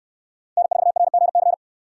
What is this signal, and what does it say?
Morse code, heard as a keyed tone, about 68 wpm, 700 Hz. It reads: N4RGC